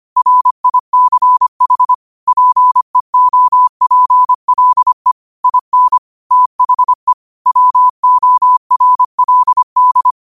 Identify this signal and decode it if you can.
Morse code, heard as a keyed tone, about 25 wpm, 1000 Hz. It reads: RICH PEOPLE IN THE WORLD